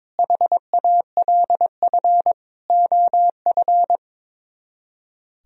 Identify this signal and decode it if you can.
Morse code, heard as a keyed tone, about 22 wpm, 700 Hz. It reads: HALF OF
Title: Speed-Racing - Sets of 5 Words 22wpm